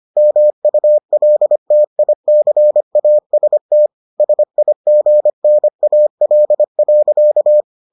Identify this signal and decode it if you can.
Morse code, heard as a keyed tone, about 25 wpm, 600 Hz. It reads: MULTICAST SIGNAL.